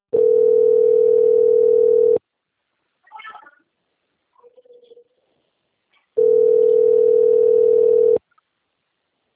Ringback tone (North America), 440 + 480 Hz.
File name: out- - - - - .6259